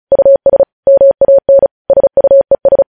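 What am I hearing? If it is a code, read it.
Morse code, heard as a keyed tone, about 36 wpm, 550 Hz. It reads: US MAN SUES
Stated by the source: News headlines sent in Morse at 35 wpm.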